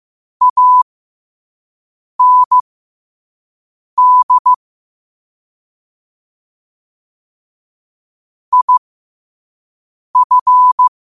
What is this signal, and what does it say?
Morse code, heard as a keyed tone, about 15 wpm, spaced as for 5 wpm, 1000 Hz. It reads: AND IF
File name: qotdfiles / 5wpm